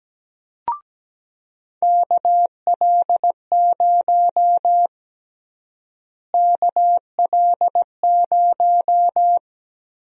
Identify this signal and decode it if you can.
Morse code, heard as a keyed tone, about 17 wpm, 700 Hz. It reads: KL0 KL0
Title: USA Call Sign Prefixes - Repeated 2x - 17wpm